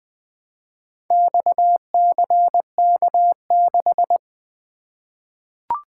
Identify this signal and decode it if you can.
Morse code, heard as a keyed tone, about 20 wpm, 700 Hz. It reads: XCK6
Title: Four Letter-Number Combinations - 20wpm